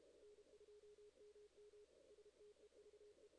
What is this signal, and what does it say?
Morse code, heard as a keyed tone, about 32 wpm, 440 Hz, with noise. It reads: N2MM STEVE
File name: 2016 JAN CW NAQP / 2016 Jan naqp k5ot - 0-11